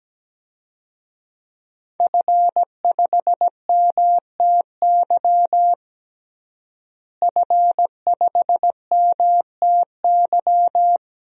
Morse code, heard as a keyed tone, about 17 wpm, 700 Hz. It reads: F5MTY F5MTY
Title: DXCC-100 Call Signs - 5 Characters - Repeated 3x - 17wpm